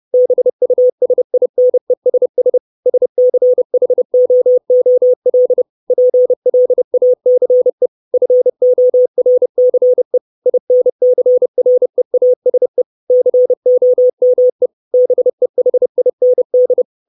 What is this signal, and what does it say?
Morse code, heard as a keyed tone, about 30 wpm, 500 Hz. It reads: BUSINESS SCHOOL PLACE FORCE INCREASE COME BEHIND
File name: 632 Words 30WPM